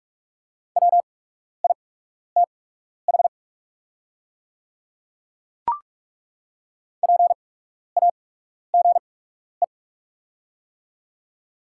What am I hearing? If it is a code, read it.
Morse code, heard as a keyed tone, about 47 wpm, spaced as for 12 wpm, 690 Hz. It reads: WITH PAGE
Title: 8x Character Spacing - Top 500 Words - 45wpm